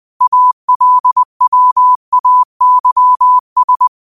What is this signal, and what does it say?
Morse code, heard as a keyed tone, about 20 wpm, 1000 Hz. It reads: ALWAYS
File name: qotdfiles / 20wpm